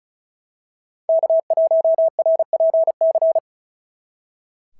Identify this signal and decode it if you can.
Morse code, heard as a keyed tone, about 34 wpm, 690 Hz. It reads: K1RPC
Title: SOTA QSO Practice 35wpm